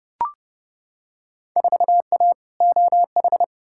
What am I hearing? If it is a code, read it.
Morse code, heard as a keyed tone, about 32 wpm, 700 Hz. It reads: 4A OH
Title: Field Day Exchanges - 30wpm